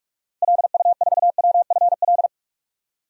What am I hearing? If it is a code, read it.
Morse code, heard as a keyed tone, about 46 wpm, 700 Hz. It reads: LU4WFL
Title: DXCC-100 Call Signs - 6 Characters - Repeated 2x - 45wpm